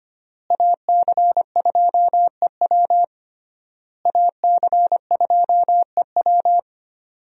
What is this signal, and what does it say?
Morse code, heard as a keyed tone, about 25 wpm, 700 Hz. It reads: AC2EW AC2EW
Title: USA Call Signs - 5 Characters - Repeated 3x - 25wpm